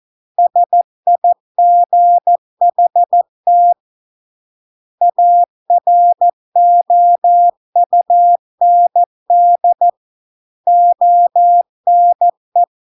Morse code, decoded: SIGHT AROUND ONE

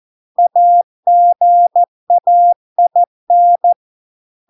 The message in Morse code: AGAIN

